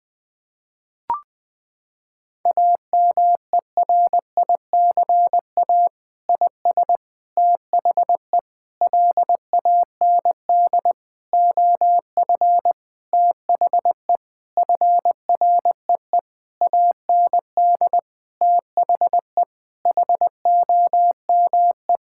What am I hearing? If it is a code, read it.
Morse code, heard as a keyed tone, about 21 wpm, 700 Hz. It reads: AMERICA IS THE LAND OF THE FREE AND THE HOME
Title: Sentences with 8-Word Phrases - 20wpm